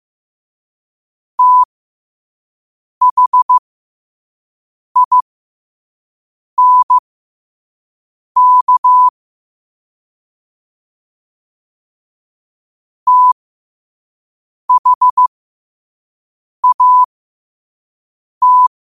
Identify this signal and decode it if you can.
Morse code, heard as a keyed tone, about 15 wpm, spaced as for 5 wpm, 1000 Hz. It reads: THINK THAT
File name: qotdfiles / 5wpm